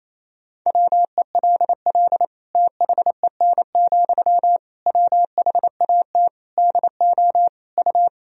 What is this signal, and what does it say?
Morse code, heard as a keyed tone, about 27 wpm, 700 Hz. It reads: WELL THEN, WHAT DO U
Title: Sentences from Top 200 Words and Abbreviations 28wpm